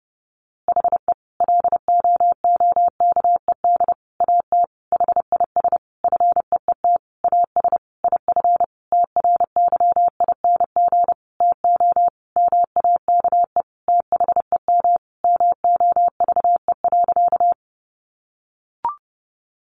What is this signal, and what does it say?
Morse code, heard as a keyed tone, about 30 wpm, 700 Hz. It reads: HE LOOKED AT HIS FEET AS IF TRYING TO MAKE THEM MOVE.
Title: A Sound Of Thunder 30wpm